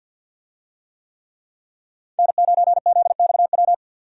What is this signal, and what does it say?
Morse code, heard as a keyed tone, about 47 wpm, 700 Hz. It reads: N9ZXW
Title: USA Call Signs - 5 Characters - 50wpm